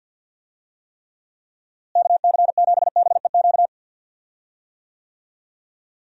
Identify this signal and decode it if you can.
Morse code, heard as a keyed tone, about 47 wpm, 700 Hz. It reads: KC7BEQ
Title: USA Call Signs - 6 Characters - Repeated 2x - 50wpm